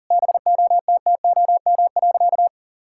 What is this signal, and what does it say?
Morse code, heard as a keyed tone, about 37 wpm, 700 Hz. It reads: BOTTOM.
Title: Sentences from Top 1000 Words - Review Part 6 - 40wpm